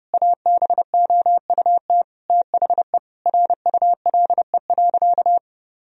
Morse code, heard as a keyed tone, about 32 wpm, 700 Hz. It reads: ABOUT THE RULE.
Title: Sentences from Top 1000 Words - Review Part 2 - 30wpm